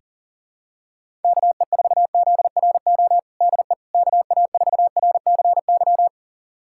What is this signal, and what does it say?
Morse code, heard as a keyed tone, about 40 wpm, 700 Hz. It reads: KE4ZRO DE KA4RCY
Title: USA Callsign DE Callsign 40wpm